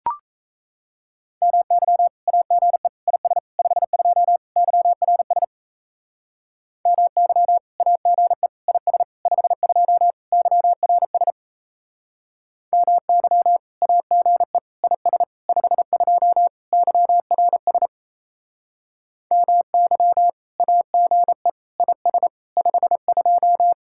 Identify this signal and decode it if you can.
Morse code, heard as a keyed tone, about 36 wpm, 700 Hz. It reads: MY AGE IS 52 YRS MY AGE IS 52 YRS MY AGE IS 52 YRS MY AGE IS 52